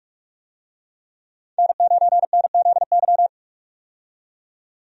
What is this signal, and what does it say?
Morse code, heard as a keyed tone, about 46 wpm, 700 Hz. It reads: N9NZY
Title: USA Call Signs - 5 Characters - 45wpm